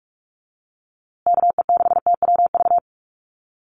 Morse code, heard as a keyed tone, about 44 wpm, 700 Hz. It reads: KE6TWV
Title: USA Call Signs Repeated 2x 45wpm